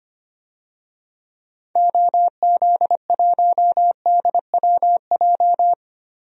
Morse code, heard as a keyed tone, about 25 wpm, 700 Hz. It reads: OZ1DWJ
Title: DXCC-100 Call Signs - 6 Characters - 25wpm